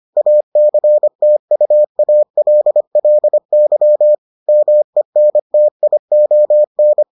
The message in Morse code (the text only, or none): ACTUALLY MENTION